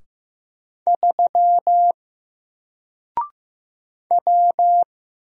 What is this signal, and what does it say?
Morse code, heard as a keyed tone, about 15 wpm, 700 Hz. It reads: 3 W
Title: Single Letter-Number 15wpm